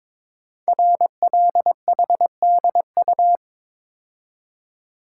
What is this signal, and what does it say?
Morse code, heard as a keyed tone, about 22 wpm, 700 Hz. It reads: RLHDU
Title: Code Groups - 5 Letters - 22wpm